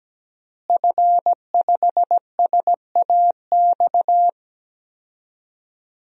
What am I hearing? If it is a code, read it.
Morse code, heard as a keyed tone, about 17 wpm, 700 Hz. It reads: F5SAX
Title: DXCC-100 Call Signs - 5 Characters - Repeated 3x - 17wpm